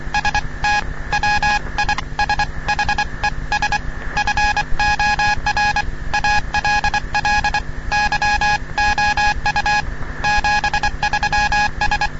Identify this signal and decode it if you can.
Morse code, heard as a keyed tone, about 25 wpm, 830 Hz, with noise. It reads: ST WISHES FOR ALL YOU 73S
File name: europe / bel / bel-265-ostooestenderadioclg-g-240-gldlsndsend-1997-ll